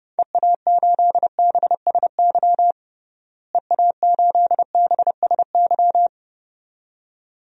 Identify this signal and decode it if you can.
Morse code, heard as a keyed tone, about 30 wpm, 700 Hz. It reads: EA8BSY EA8BSY
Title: DXCC-100 Call Signs - 6 Characters - Repeated 2x - 30wpm